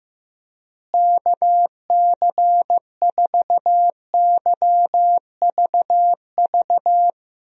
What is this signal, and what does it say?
Morse code, heard as a keyed tone, about 15 wpm, 700 Hz. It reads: KC4YVV
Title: USA Call Signs - 6 Characters - Repeated 3x - 15wpm